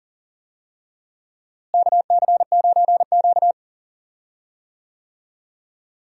Morse code, heard as a keyed tone, about 40 wpm, 700 Hz. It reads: KC9Q
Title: USA Call Signs - 4 Characters - Repeated 3x - 40wpm